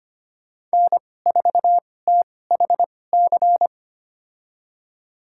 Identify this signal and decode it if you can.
Morse code, heard as a keyed tone, about 25 wpm, spaced as for 18 wpm, 700 Hz. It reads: N4THC
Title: USA Call Signs - 5 Characters - 2x Character Spacing - 25wpm